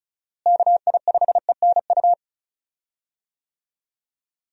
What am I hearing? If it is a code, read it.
Morse code, heard as a keyed tone, about 36 wpm, 700 Hz. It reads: KI5ENU
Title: USA Call Signs - 6 Characters - Repeated 2x - 35wpm